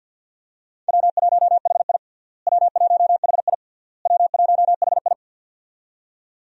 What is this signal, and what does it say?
Morse code, heard as a keyed tone, about 49 wpm, 700 Hz. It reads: W1HI W1HI W1HI
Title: USA Call Signs - 4 Characters - Repeated 3x - 50wpm